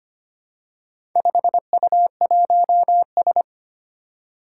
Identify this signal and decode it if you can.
Morse code, heard as a keyed tone, about 25 wpm, 700 Hz. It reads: 5U1S
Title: Four Challenging Letter-Number Combinations 25wpm